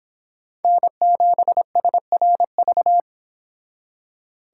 Morse code, heard as a keyed tone, about 31 wpm, 700 Hz. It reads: N7SRV